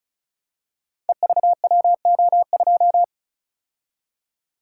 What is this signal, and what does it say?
Morse code, heard as a keyed tone, about 36 wpm, 700 Hz. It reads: EVWO2
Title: Code Groups - 5 Letter:Numbers - 35wpm